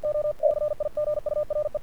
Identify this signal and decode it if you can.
Morse code, heard as a keyed tone, about 48 wpm, 590 Hz. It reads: O CIGURE
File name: archived / CRSnet